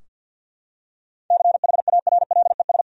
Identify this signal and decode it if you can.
Morse code, heard as a keyed tone, about 47 wpm, 700 Hz. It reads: CHARLES
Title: USA Common Names - 50wpm